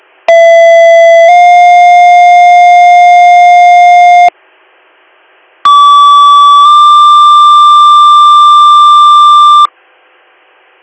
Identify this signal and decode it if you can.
Two-tone page, heard as a one-second tone, then a three-second tone, about 670 then 710 Hz, about 1150 then 1180 Hz. It is heard from 0.3 to 9.7 s.